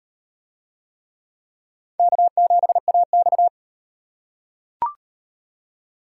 Morse code, heard as a keyed tone, about 38 wpm, 700 Hz. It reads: K7AX